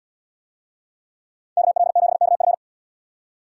Morse code, heard as a keyed tone, about 76 wpm, 700 Hz. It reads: KL7KF